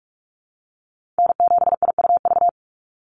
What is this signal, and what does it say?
Morse code, heard as a keyed tone, about 46 wpm, 690 Hz. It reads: N7IUV
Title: USA Call Signs 45wpm